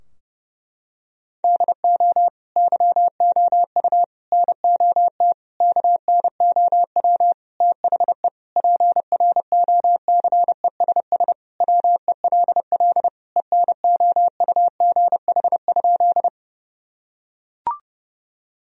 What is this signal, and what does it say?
Morse code, heard as a keyed tone, about 30 wpm, 700 Hz. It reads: DO YOU NOT KNOW THE PROCESS WELL ENOUGH?
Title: Sentences from Top 1000 Words - Review Part 3 - 30wpm